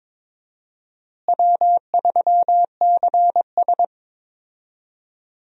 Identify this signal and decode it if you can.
Morse code, heard as a keyed tone, about 22 wpm, 700 Hz. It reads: W3CS
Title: USA Call Signs - 4 Characters - Repeated 2x - 22wpm